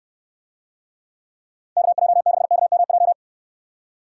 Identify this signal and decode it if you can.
Morse code, heard as a keyed tone, about 63 wpm, 700 Hz. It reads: KJ6KMJ